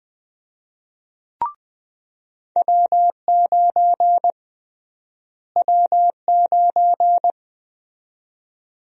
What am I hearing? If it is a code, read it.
Morse code, heard as a keyed tone, about 20 wpm, 700 Hz. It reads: W9 W9
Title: USA Call Sign Prefixes - Repeated 2x - 20wpm